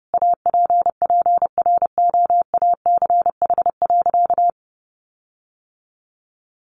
Morse code, heard as a keyed tone, about 30 wpm, 700 Hz. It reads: APPROACH.
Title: Sentences from Top 1400 Words 30wpm